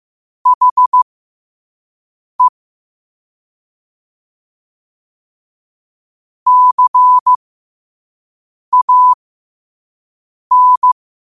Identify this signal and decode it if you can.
Morse code, heard as a keyed tone, about 15 wpm, spaced as for 5 wpm, 1000 Hz. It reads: HE CAN
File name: qotdfiles / 5wpm